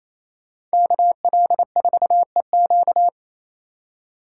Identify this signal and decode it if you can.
Morse code, heard as a keyed tone, about 33 wpm, 700 Hz. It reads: KL4EQ
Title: Speed Racing - USA 2x2 Call Signs 28wpm